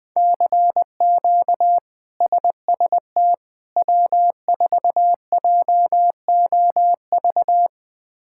Morse code, heard as a keyed tone, about 21 wpm, 700 Hz. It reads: CQ SST W4JOV
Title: K1USN SST QSO Practice - 20wpm